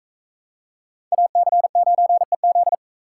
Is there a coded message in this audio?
Yes: Morse code from 1.1 to 2.7 s, heard as a keyed tone, about 38 wpm, 700 Hz. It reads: AC9EZ